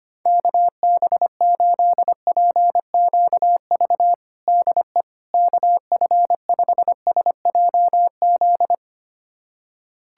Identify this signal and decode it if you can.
Morse code, heard as a keyed tone, about 25 wpm, 700 Hz. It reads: KB8PQV DE KF5SJZ
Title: USA Callsign DE Callsign - 6 Characters - 25wpm